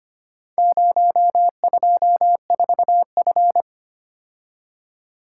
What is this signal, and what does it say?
Morse code, heard as a keyed tone, about 25 wpm, 700 Hz. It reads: 024F
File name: practice / 25wpm / Four Letter-Number Combinations - 25wpm